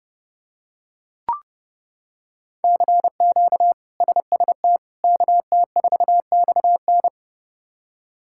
Morse code, heard as a keyed tone, about 30 wpm, 700 Hz. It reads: CQ SST KT4XN
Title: K1USN SST QSO Practice - 30wpm